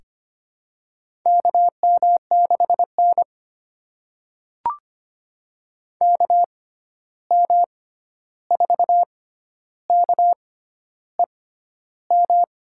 Morse code, heard as a keyed tone, about 25 wpm, spaced as for 11 wpm, 700 Hz. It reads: KM6N KM4KEM
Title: USA Call Signs - 6x Character Spacing - 25wpm